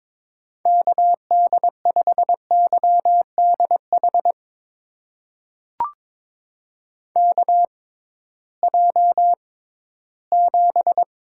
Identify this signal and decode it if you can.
Morse code, heard as a keyed tone, about 22 wpm, spaced as for 14 wpm, 700 Hz. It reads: KD5YDH KJ7
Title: USA Call Signs - 6 Characters - 6x Character Spacing - 22wpm